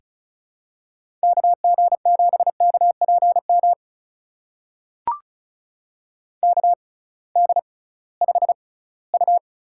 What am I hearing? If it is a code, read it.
Morse code, heard as a keyed tone, about 36 wpm, spaced as for 20 wpm, 700 Hz. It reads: KG7KPM KD5U
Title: USA Call Signs - 6 Characters - 6x Character Spacing - 35wpm